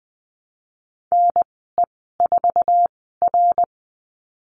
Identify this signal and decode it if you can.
Morse code, heard as a keyed tone, about 19 wpm, spaced as for 14 wpm, 700 Hz. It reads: NE4R
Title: 128 Lesson 21g - Number 6 - Call Signs 20wpm